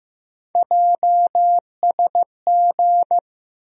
Morse code, heard as a keyed tone, about 15 wpm, 700 Hz. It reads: JSG
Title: USA Call Sign Postfixes - 3 Letters 15wpm